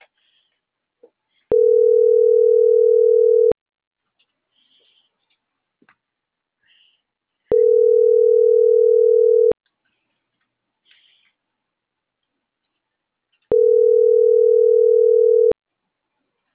Ringback tone (North America), 440 + 480 Hz.